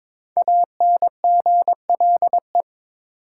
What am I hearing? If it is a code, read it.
Morse code, heard as a keyed tone, about 22 wpm, 700 Hz. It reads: ANGLE
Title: Sentences from Top 1500 Words - New Word Drill Alt - 22wpm